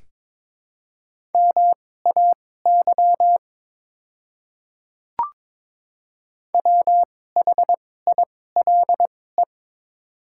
Morse code, heard as a keyed tone, about 22 wpm, spaced as for 16 wpm, 700 Hz. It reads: MAY WHILE
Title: 2x Character Spacing - Top 500 Words - 22wpm